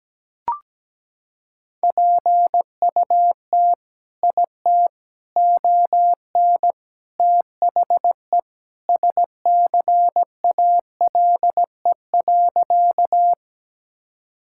Morse code, heard as a keyed tone, about 17 wpm, 700 Hz. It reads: PUT IT ON THE SCALE.
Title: Sentences from Top 1000 Words - Review Part 5 - 17wpm